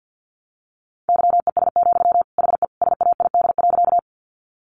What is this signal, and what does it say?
Morse code, heard as a keyed tone, about 50 wpm, 700 Hz. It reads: YES, HE SAID.
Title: Sentences from Top 500 Words 50wpm